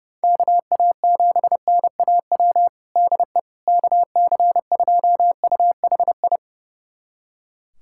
Morse code, heard as a keyed tone, about 30 wpm, 700 Hz. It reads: KA7NAW DE KC2UHI
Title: USA Callsign DE Callsign 30wpm